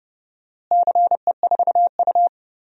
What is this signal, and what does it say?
Morse code, heard as a keyed tone, about 30 wpm, 700 Hz. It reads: CE4U
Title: DXCC-100 Call Signs - 4 Characters - Repeated 2x - 30wpm